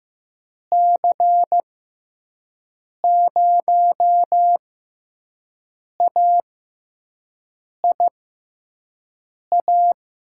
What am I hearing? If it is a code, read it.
Morse code, heard as a keyed tone, about 15 wpm, spaced as for 5 wpm, 700 Hz. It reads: C0AIA
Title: USA Call Signs - 6 Characters - 6x Character Spacing - 15wpm